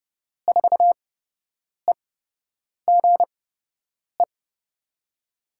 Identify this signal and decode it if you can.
Morse code, heard as a keyed tone, about 30 wpm, spaced as for 8 wpm, 700 Hz. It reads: 4EGE
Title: USA Call Signs - 6 Characters - 8x Character Spacing - 30wpm